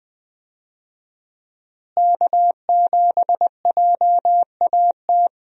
Morse code, heard as a keyed tone, about 20 wpm, 700 Hz. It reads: K7JAT
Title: USA Call Signs - 5 Characters - Repeated 2x - 20wpm